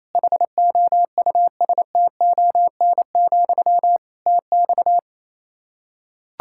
Morse code, heard as a keyed tone, about 27 wpm, 700 Hz. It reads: HOUSTON, TX
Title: K1USN SST QSO Practice 28wpm